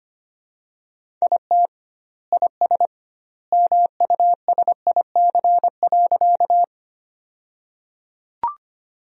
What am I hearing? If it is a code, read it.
Morse code, heard as a keyed tone, about 25 wpm, 700 Hz. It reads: IT IS MUSIC.
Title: 2x Word-Spacing - Sentences from Top 300 Words - 25wpm